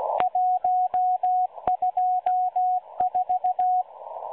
Morse code, heard as a keyed tone, about 16 wpm, 730 Hz, with radio noise. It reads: 124
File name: M12-Morse-Numbers-Station-8116-kHz-CW